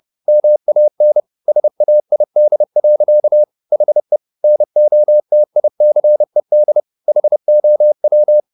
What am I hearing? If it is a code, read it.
Morse code, heard as a keyed tone, about 30 wpm, 600 Hz. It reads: MAN SAID. HE NOTICED HOW